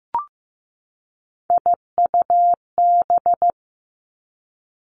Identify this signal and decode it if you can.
Morse code, heard as a keyed tone, about 15 wpm, 700 Hz. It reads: IUB
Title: Three Challenging Letter-Number Combinations 15wpm